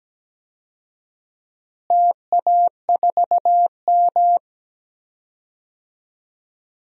Morse code, heard as a keyed tone, about 17 wpm, 700 Hz. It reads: TA4M
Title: DXCC-100 Call Signs - 4 Characters - Repeated 3x - 17wpm